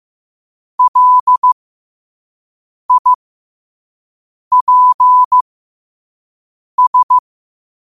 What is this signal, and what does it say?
Morse code, heard as a keyed tone, about 15 wpm, spaced as for 5 wpm, 1000 Hz. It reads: LIPS
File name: 5wpm